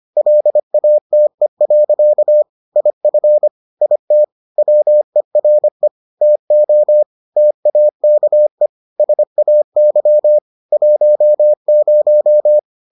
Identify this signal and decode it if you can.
Morse code, heard as a keyed tone, about 25 wpm, 600 Hz. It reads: LATE. IF IT WERE TO TAKE SAY 10